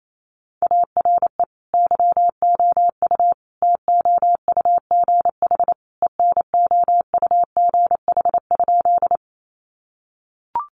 Morse code, heard as a keyed tone, about 28 wpm, 700 Hz. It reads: ARE YOU TOUGH ENOUGH?
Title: Sentences from Top 1200 Words 28wpm